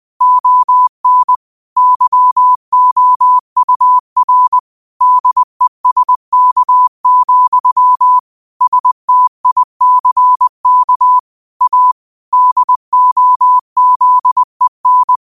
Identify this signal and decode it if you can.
Morse code, heard as a keyed tone, about 20 wpm, 1000 Hz. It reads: ON YOUR DESK, STICK A DOZEN